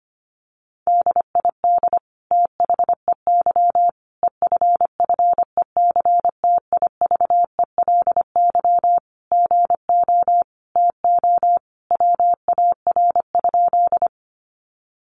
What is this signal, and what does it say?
Morse code, heard as a keyed tone, about 25 wpm, 700 Hz. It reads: DID THEY EFFECTIVELY GO TO WAR?
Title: Sentences from Top 1400 Words 25wpm